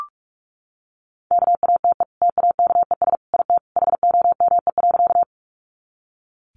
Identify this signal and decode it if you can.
Morse code, heard as a keyed tone, about 44 wpm, 700 Hz. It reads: KATE TAKES IT HOME.
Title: Easy Phrases 45wpm